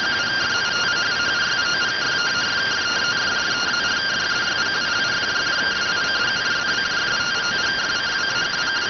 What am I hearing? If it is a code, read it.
Two-tone teleprinter signal, about 170 Hz shift.